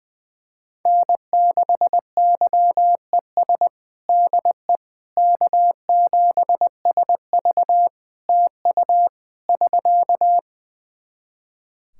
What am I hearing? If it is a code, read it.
Morse code, heard as a keyed tone, about 19 wpm, 700 Hz. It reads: N6YES DE K7SV TU <SK>
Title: K1USN SST QSO Practice 20wpm